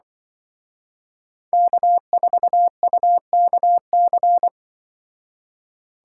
Morse code, heard as a keyed tone, about 24 wpm, 700 Hz. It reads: K4UKC
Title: Speed Racing - USA 1x3 Call Signs 20wpm